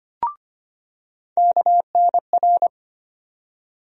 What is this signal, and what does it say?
Morse code, heard as a keyed tone, about 25 wpm, 700 Hz. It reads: KNR
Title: USA Call Sign Postfixes - 3 Letters 25wpm